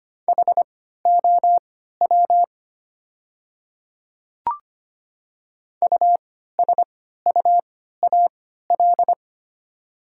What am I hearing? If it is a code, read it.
Morse code, heard as a keyed tone, about 25 wpm, spaced as for 14 wpm, 700 Hz. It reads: HOW USUAL
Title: 3x Character Spacing - Top 500 Words - 25wpm